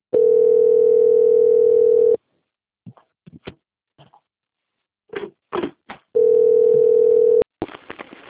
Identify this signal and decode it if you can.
Ringback tone (North America), 440 + 480 Hz.
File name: out- - - - - .33872